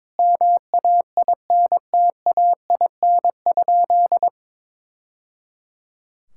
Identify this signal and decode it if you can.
Morse code, heard as a keyed tone, about 22 wpm, 700 Hz. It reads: MAINTAIN?
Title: Sentences from Top 1000 Words - Review Part 5 - 22wpm